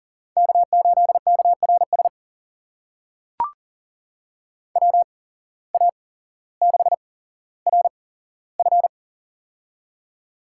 Morse code, heard as a keyed tone, about 40 wpm, spaced as for 17 wpm, 700 Hz. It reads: K8KRS WA6RF
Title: USA Call Signs - 5 Characters - 8x Character Spacing - 40wpm